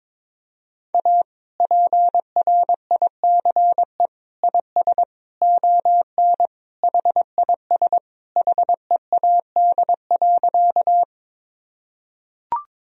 Morse code, heard as a keyed tone, about 22 wpm, 700 Hz. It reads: A PRICE IS ON HIS HEAD.
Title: Sentences from Top 1000 Words - Review Part 4 - 22wpm